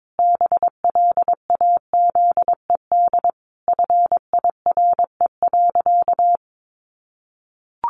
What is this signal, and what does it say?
Morse code, heard as a keyed tone, about 22 wpm, 700 Hz. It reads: BLAZED FIRE.
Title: A Sound Of Thunder 22wpm